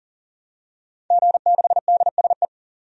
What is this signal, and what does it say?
Morse code, heard as a keyed tone, about 40 wpm, 700 Hz. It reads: G6DSE
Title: DXCC-100 Call Signs - 5 Characters - 40wpm